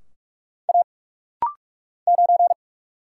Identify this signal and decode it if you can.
Morse code, heard as a keyed tone, about 46 wpm, 700 Hz. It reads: A 9
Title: Single Letter-Number Alt - Rapid-Fire 45wpm